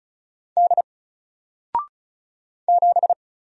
Morse code, heard as a keyed tone, about 35 wpm, 700 Hz. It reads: D 7
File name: Single-Letter-Number-35wpm